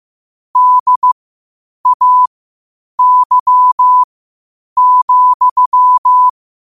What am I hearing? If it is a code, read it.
Morse code, heard as a keyed tone, about 15 wpm, spaced as for 8 wpm, 1000 Hz. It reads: DAY,
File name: qotdfiles / 8wpm